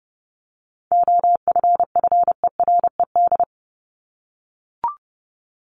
Morse code, heard as a keyed tone, about 30 wpm, 700 Hz. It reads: OFFERED
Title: ED Suffix - Top 100 Words with Base Words 30wpm